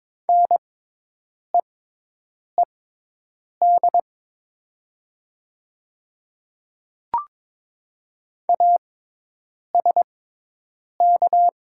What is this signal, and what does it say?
Morse code, heard as a keyed tone, about 22 wpm, spaced as for 8 wpm, 700 Hz. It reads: NEED ASK
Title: 6x Character Spacing - Top 200 Words - 22wpm